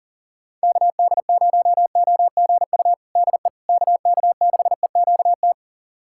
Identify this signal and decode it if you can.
Morse code, heard as a keyed tone, about 37 wpm, 700 Hz. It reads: KD0OGU DE KK6EQT
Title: USA Callsign DE Callsign - 6 Characters - 40wpm